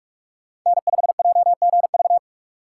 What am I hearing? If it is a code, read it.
Morse code, heard as a keyed tone, about 46 wpm, 700 Hz. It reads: N5JGV
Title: USA Call Signs - 3x Character Spacing - 45wpm